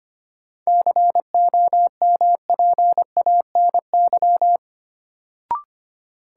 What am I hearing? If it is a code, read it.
Morse code, heard as a keyed tone, about 25 wpm, 700 Hz. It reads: COMPANY
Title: Sets of 2 Words - New Word Drill - 25wpm